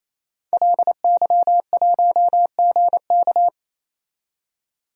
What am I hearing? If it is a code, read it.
Morse code, heard as a keyed tone, about 27 wpm, 700 Hz. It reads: LY1GK
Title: DXCC-100 Call Signs - 5 Characters - 28wpm